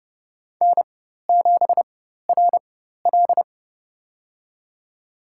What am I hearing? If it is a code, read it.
Morse code, heard as a keyed tone, about 30 wpm, spaced as for 14 wpm, 700 Hz. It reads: N7RL
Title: USA Call Signs - 4 Characters - 4x Character Spacing - 30wpm